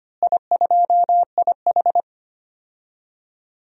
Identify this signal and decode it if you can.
Morse code, heard as a keyed tone, about 25 wpm, 700 Hz. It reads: I2IH
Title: DXCC-100 Call Signs - 4 Characters - Repeated 2x - 25wpm